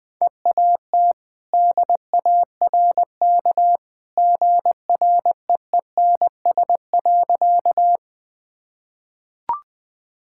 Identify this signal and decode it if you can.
Morse code, heard as a keyed tone, about 21 wpm, 700 Hz. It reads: EAT DARK GREENS.
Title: Easy Sentences with Hints - 20wpm